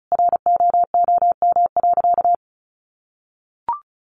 Morse code, heard as a keyed tone, about 34 wpm, 700 Hz. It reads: ROOM.
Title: Easy Phrases 35wpm